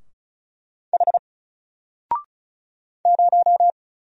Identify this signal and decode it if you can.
Morse code, heard as a keyed tone, about 36 wpm, 700 Hz. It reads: H 0
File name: Single-Letter-Number-35wpm